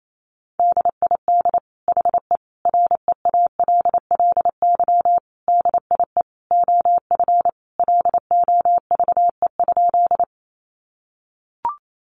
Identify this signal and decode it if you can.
Morse code, heard as a keyed tone, about 28 wpm, 700 Hz. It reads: DID HE REALLY DIE OF LOVE?
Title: Sentences from Top 1300 Words 28wpm